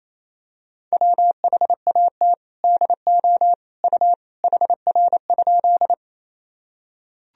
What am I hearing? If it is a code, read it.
Morse code, heard as a keyed tone, about 28 wpm, 700 Hz. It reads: WHAT DO U HR?
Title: Sentences from Top 300 Words and Abbreviations 28wpm